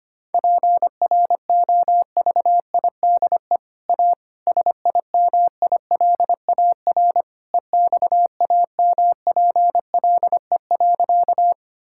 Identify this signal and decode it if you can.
Morse code, heard as a keyed tone, about 25 wpm, 700 Hz. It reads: PROVIDE A SIMILAR EXAMPLE.
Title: Sentences from Top 1000 Words - Review Part 5 - 25wpm